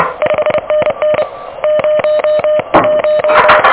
Morse code, heard as a keyed tone, about 27 wpm, 600 Hz, with noise. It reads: 5NN 01
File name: OK QRP 2023 / OK QRP 2023 OK1DWF